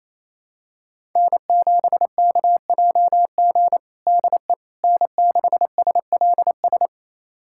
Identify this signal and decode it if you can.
Morse code, heard as a keyed tone, about 28 wpm, 700 Hz. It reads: N7KJG DE N6SLS
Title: USA Callsign DE Callsign - 5 Characters - 28wpm